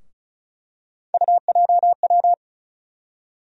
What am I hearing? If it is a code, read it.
Morse code, heard as a keyed tone, about 34 wpm, 700 Hz. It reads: UJW
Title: USA Call Sign Postfixes - 3 Letters 35wpm